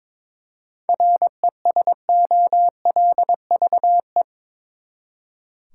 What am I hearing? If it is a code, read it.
Morse code, heard as a keyed tone, about 22 wpm, 700 Hz. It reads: RESOLVE
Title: Seven Letter Words - 22wpm